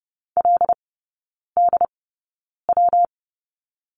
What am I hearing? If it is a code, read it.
Morse code, heard as a keyed tone, about 30 wpm, spaced as for 9 wpm, 700 Hz. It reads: LDW
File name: Lesson 19e - Letter w - Three Characters 30wpm